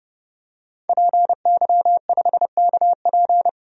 Morse code, heard as a keyed tone, about 32 wpm, 700 Hz. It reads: PY5KP
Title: DXCC-100 Call Signs - 5 Characters - 30wpm